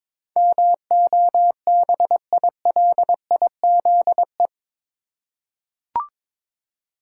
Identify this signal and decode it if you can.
Morse code, heard as a keyed tone, about 22 wpm, 700 Hz. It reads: MOBILIZE
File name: practice / 22wpm / Eight Letter Words - 22wpm